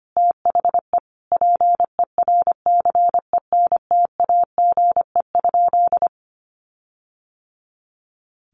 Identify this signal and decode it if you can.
Morse code, heard as a keyed tone, about 25 wpm, 700 Hz. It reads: THE PERCENTAGE?
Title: Sentences from Top 1300 Words 25wpm